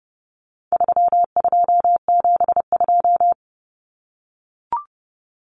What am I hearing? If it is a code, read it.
Morse code, heard as a keyed tone, about 30 wpm, 700 Hz. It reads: 3272
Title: Common Cut Numbers - 4-Digit 30wpm